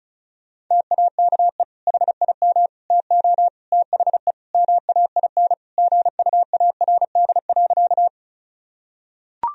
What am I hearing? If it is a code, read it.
Morse code, heard as a keyed tone, about 36 wpm, 700 Hz. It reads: TAKE HIM TO THE MAIN GUARD.
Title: Sentences from Top 1000 Words - Review Part 4 - 35wpm